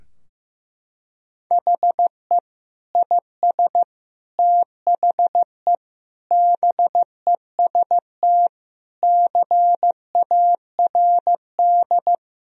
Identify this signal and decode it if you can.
Morse code, heard as a keyed tone, about 15 wpm, 700 Hz. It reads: HE IS THE BEST CARD
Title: Sentences from Top 1000 Words - Review Part 4 - 15wpm